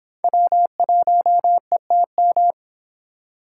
Morse code, heard as a keyed tone, about 26 wpm, 700 Hz. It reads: W1ETM